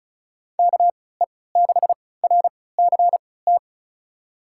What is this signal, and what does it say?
Morse code, heard as a keyed tone, about 36 wpm, spaced as for 20 wpm, 700 Hz. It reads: KE6RCT
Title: USA Call Signs - 6 Characters - 3x Character Spacing - 35wpm